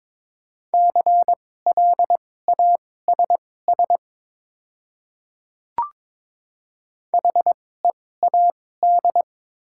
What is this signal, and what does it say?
Morse code, heard as a keyed tone, about 21 wpm, spaced as for 16 wpm, 700 Hz. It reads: CLASS HEAD
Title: 2x Character Spacing - Top 500 Words - 22wpm